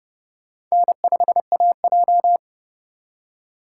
Morse code, heard as a keyed tone, about 36 wpm, 700 Hz. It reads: N5AJ
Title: Speed Racing - USA 1x2 Call Signs 30wpm